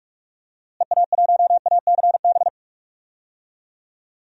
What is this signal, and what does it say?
Morse code, heard as a keyed tone, about 44 wpm, 700 Hz. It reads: EA1ACB
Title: DXCC-100 Call Signs - 6 Characters - Repeated 3x - 45wpm